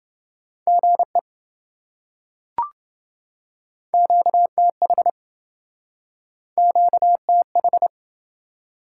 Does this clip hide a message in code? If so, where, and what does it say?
Yes: Morse code from 0.7 to 7.9 s, heard as a keyed tone, about 28 wpm, 700 Hz. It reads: GE QTH QTH